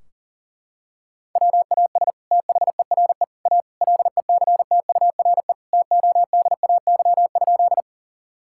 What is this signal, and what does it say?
Morse code, heard as a keyed tone, about 40 wpm, 700 Hz. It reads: WAS THERE A LECTURE TODAY?
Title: Sentences from Top 1600 Words 40wpm